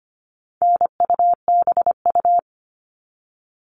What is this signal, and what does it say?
Morse code, heard as a keyed tone, about 25 wpm, 700 Hz. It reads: NUBU
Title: All Cut Numbers - 4-Digit 25wpm